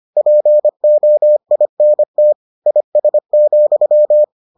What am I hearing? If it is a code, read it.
Morse code, heard as a keyed tone, about 25 wpm, 600 Hz. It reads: POINT IS,